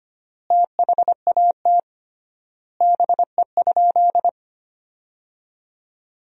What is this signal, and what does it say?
Morse code, heard as a keyed tone, about 25 wpm, 700 Hz. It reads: THAT BE?
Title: 3x Word-Spacing - Sentences from Top 100 Words - 25wpm